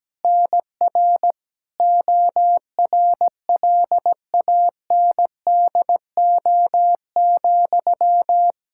Morse code, heard as a keyed tone, about 17 wpm, 700 Hz. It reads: NR ORLANDO,
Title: Practice QSO - Exchange 1 - Beginning - 17wpm